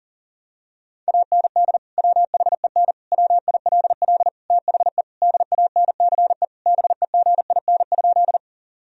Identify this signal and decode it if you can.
Morse code, heard as a keyed tone, about 40 wpm, 700 Hz. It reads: AND WHEN WILL THE DANCE BEGIN?
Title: Sentences from Top 1000 Words - Review Part 4 - 40wpm